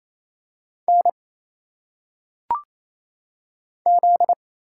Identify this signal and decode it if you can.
Morse code, heard as a keyed tone, about 27 wpm, 700 Hz. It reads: N Z
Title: Single Letters - Phonetic Pronunciation 28wpm